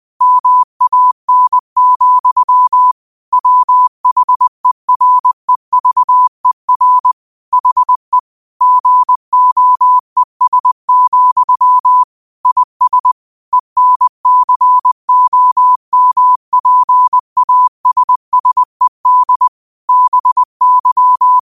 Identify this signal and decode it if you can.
Morse code, heard as a keyed tone, about 20 wpm, 1000 Hz. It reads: MAN, WHEREVER HE GOES, IS ENCOMPASSED BY
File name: qotdfiles / 20wpm